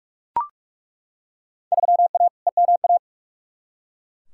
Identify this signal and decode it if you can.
Morse code, heard as a keyed tone, about 42 wpm, 700 Hz. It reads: 3A EMA
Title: Field Day Exchanges - 45wpm